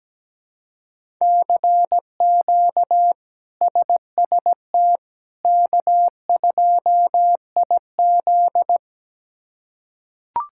Morse code, heard as a keyed tone, about 17 wpm, 700 Hz. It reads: CQ SST K2IZ
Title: K1USN SST QSO Practice - 17wpm